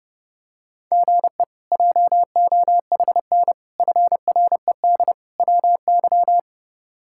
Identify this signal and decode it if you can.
Morse code, heard as a keyed tone, about 30 wpm, 700 Hz. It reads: GE JOHN FRED WY
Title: K1USN SST QSO Practice - 30wpm